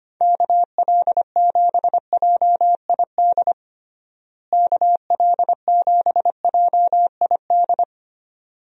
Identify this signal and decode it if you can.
Morse code, heard as a keyed tone, about 25 wpm, 700 Hz. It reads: KL7JID KL7JID
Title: DXCC-100 Call Signs - 6 Characters - Repeated 2x - 25wpm